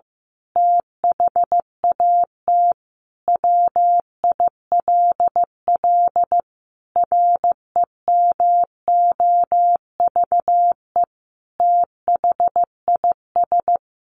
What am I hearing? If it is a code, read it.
Morse code, heard as a keyed tone, about 15 wpm, 700 Hz. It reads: THAT WILL REMOVE THIS
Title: Sentences from Top 1500 Words 15wpm